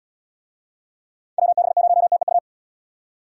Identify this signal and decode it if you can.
Morse code, heard as a keyed tone, about 74 wpm, 700 Hz. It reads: WB0TEU